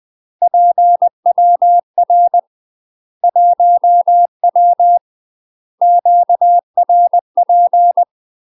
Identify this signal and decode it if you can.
Morse code, heard as a keyed tone, about 20 wpm, 700 Hz. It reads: PWR 1W QRP
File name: UK MASTER QSOs TRAINING / 20WPM / UK MASTER QSOs 0024